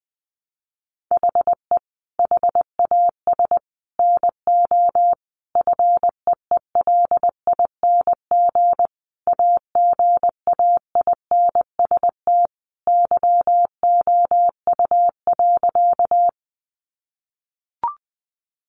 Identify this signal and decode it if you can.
Morse code, heard as a keyed tone, about 20 wpm, 700 Hz. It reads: HE HAS NO FEELING AGAINST YOU.
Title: Sentences from Top 1200 Words 20wpm